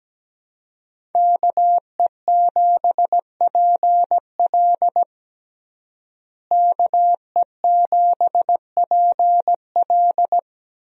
Morse code, heard as a keyed tone, about 17 wpm, 700 Hz. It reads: KE7PL KE7PL
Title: USA Call Signs - 5 Characters - Repeated 2x - 17wpm